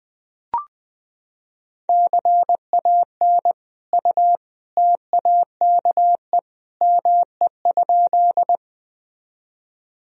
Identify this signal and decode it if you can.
Morse code, heard as a keyed tone, about 20 wpm, 700 Hz. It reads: CAN U TAKE ME?
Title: Sentences from Top 200 Words and Abbreviations 20wpm